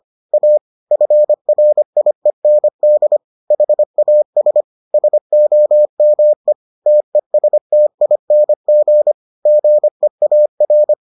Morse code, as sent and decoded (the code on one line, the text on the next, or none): .- / ..-. .-. .. -. -.. / .... .- ... / ... --- -- . / - . ... - .. -. --. / --. .- .-.
A FRIEND HAS SOME TESTING GEAR